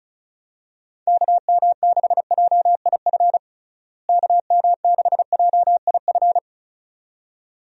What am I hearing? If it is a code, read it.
Morse code, heard as a keyed tone, about 34 wpm, 700 Hz. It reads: KM6JIF KM6JIF
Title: USA Call Signs - 6 Characters - Repeated 2x - 35wpm